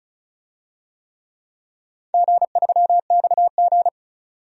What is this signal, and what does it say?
Morse code, heard as a keyed tone, about 36 wpm, 700 Hz. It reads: G3XG
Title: DXCC-100 Call Signs - 4 Characters - 35wpm